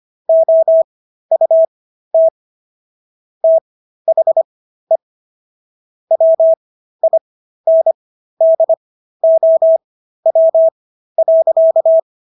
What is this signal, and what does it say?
Morse code, heard as a keyed tone, about 25 wpm, spaced as for 13 wpm, 650 Hz. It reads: OUT THE WINDOW.